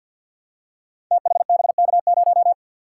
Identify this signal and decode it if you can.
Morse code, heard as a keyed tone, about 47 wpm, 700 Hz. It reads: THBK0
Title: Code Groups - 5 Letter:Numbers - 50wpm